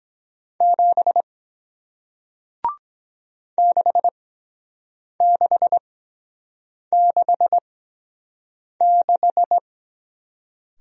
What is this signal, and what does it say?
Morse code, heard as a keyed tone, about 23 wpm, spaced as for 7 wpm, 700 Hz. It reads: 7 6666